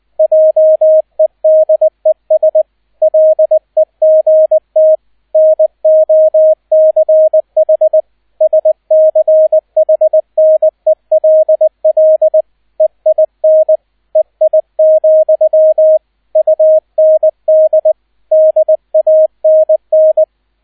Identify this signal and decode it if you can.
Morse code, heard as a keyed tone, about 20 wpm, 630 Hz, with radio noise. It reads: JEDES LEGT NOCH SCHNELL EIN EI, UND DANN
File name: Erster Streich 20wpm